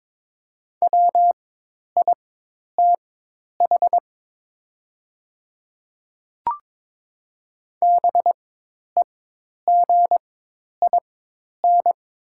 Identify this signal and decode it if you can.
Morse code, heard as a keyed tone, about 22 wpm, spaced as for 10 wpm, 700 Hz. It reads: WITH BEGIN
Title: 4x Character Spacing - Top 300 Words - 22wpm